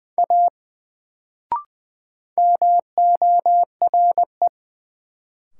Morse code, heard as a keyed tone, about 20 wpm, 700 Hz. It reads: A MORE
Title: Sentences from Top 100 Words - New Word Drill - 20wpm